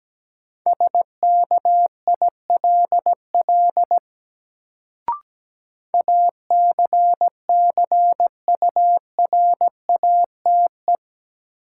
Morse code, heard as a keyed tone, about 17 wpm, 700 Hz. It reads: SKILL ACCURATE
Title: Sentences from Top 1200 Words - New Word Drill Alt - 17wpm